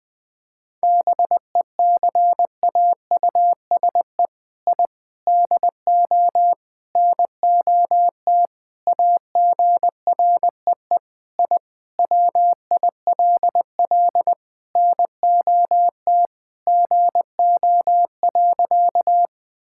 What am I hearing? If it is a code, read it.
Morse code, heard as a keyed tone, about 20 wpm, 700 Hz. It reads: BECAUSE I DO NOT AGREE I WILL NOT GO.
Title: Sentences with 4-Word Phrases - Spoken Beginning - 20wpm